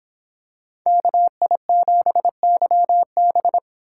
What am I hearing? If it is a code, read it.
Morse code, heard as a keyed tone, about 25 wpm, 700 Hz. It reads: KI7YB